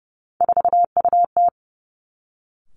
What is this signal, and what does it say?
Morse code, heard as a keyed tone, about 30 wpm, 700 Hz. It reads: 4UT
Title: Three Challenging Letter-Number Combinations 30wpm